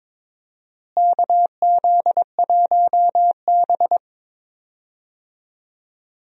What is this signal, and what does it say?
Morse code, heard as a keyed tone, about 22 wpm, 700 Hz. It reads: KZ1B
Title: USA Call Signs - 4 Characters - Repeated 3x - 22wpm